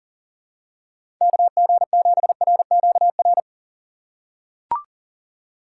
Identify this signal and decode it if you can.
Morse code, heard as a keyed tone, about 37 wpm, 700 Hz. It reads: KG7RQR
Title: USA Call Signs - 6 Characters - 3x Character Spacing - 40wpm